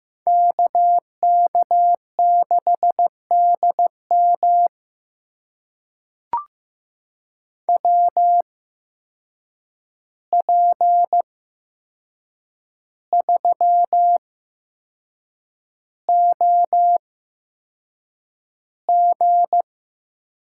Morse code, heard as a keyed tone, about 15 wpm, spaced as for 6 wpm, 700 Hz. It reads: KK6DM WP3OG
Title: USA Call Signs - 5 Characters - 8x Character Spacing - 15wpm